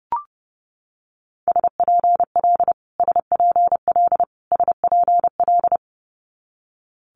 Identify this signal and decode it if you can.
Morse code, heard as a keyed tone, about 32 wpm, 700 Hz. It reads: SPL SPL SPL
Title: Multi-letter Phonemes Repeated 3x 30wpm